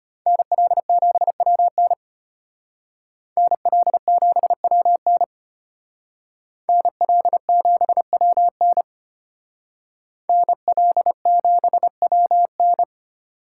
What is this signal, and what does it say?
Morse code, heard as a keyed tone, about 32 wpm, 700 Hz. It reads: NL7WN NL7WN NL7WN NL7WN